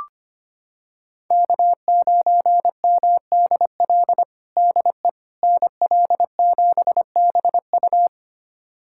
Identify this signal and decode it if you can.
Morse code, heard as a keyed tone, about 25 wpm, 700 Hz. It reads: K9MDL DE NL7BU
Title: USA Callsign DE Callsign - 5 Characters - 25wpm